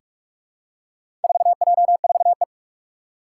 Morse code, heard as a keyed tone, about 41 wpm, 700 Hz. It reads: 4J4E
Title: Four Challenging Letter-Number Combinations 45wpm